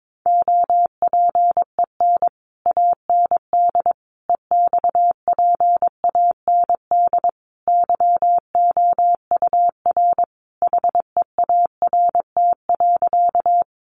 Morse code, heard as a keyed tone, about 22 wpm, 700 Hz. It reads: OPEN AND EXPAND YOUR HEART.
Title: Sentences from Top 1400 Words 22wpm